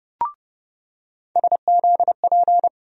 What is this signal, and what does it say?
Morse code, heard as a keyed tone, about 31 wpm, 700 Hz. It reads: SZP
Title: USA Call Sign Postfixes - 3 Letters 30wpm